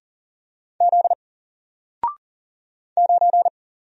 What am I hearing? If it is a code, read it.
Morse code, heard as a keyed tone, about 40 wpm, 700 Hz. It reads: Z 9
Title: Single Letter-Number 40wpm